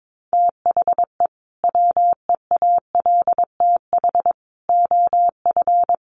Morse code, heard as a keyed tone, about 22 wpm, 700 Hz. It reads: THE WEALTH OF
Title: Sentences from Top 1200 Words 22wpm